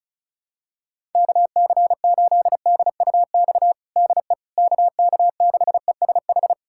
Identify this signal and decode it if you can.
Morse code, heard as a keyed tone, about 36 wpm, 700 Hz. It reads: KC8DUX DE KK6ESH
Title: USA Callsign DE Callsign 35wpm